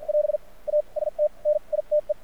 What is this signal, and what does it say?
Morse code, heard as a keyed tone, about 50 wpm, spaced as for 38 wpm, 600 Hz. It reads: <AS> AST NITE
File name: archived / CRSnet